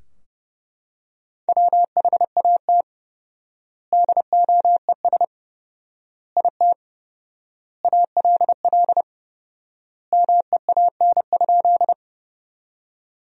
Morse code, heard as a keyed tone, about 32 wpm, 700 Hz. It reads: WHAT DOES IT ALL MEAN?
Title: 4x Word-Spacing - Sentences from Top 200 Words - 30wpm